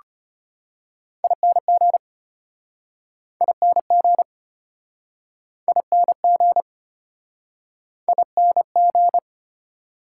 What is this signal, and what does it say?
Morse code, heard as a keyed tone, about 36 wpm, 700 Hz. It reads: ING ING ING ING